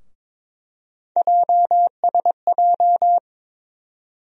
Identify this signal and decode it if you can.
Morse code, heard as a keyed tone, about 22 wpm, 700 Hz. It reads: JSJ
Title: USA Call Sign Postfixes - 3 Letters 22wpm